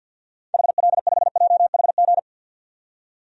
Morse code, heard as a keyed tone, about 53 wpm, 700 Hz. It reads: HL5JHG
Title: DXCC-100 Call Signs - 6 Characters - 50wpm